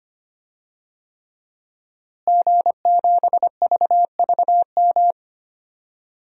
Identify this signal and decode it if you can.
Morse code, heard as a keyed tone, about 25 wpm, 700 Hz. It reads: G7VVM
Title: DXCC-100 Call Signs - 5 Characters - Repeated 3x - 25wpm